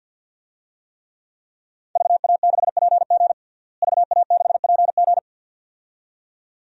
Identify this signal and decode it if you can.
Morse code, heard as a keyed tone, about 51 wpm, 700 Hz. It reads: VA6PG VA6PG
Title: DXCC-100 Call Signs - 5 Characters - Repeated 2x - 50wpm